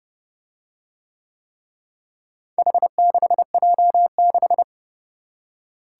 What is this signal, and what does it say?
Morse code, heard as a keyed tone, about 30 wpm, 700 Hz. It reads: H6J6